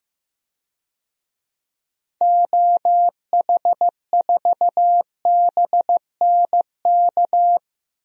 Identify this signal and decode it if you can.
Morse code, heard as a keyed tone, about 15 wpm, 690 Hz. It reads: OH4BNK
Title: DXCC-100 Call Signs - 6 Characters - 15wpm